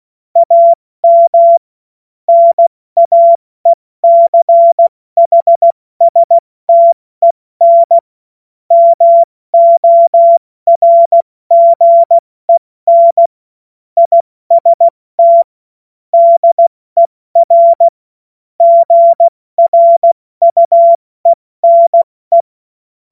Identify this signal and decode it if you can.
Morse code, heard as a keyed tone, about 16 wpm, 680 Hz. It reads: AM NAECHSTEN MORGEN IST DER GRUENE
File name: Morsekurs / Witze / Witz 28. September 2021, 70 F80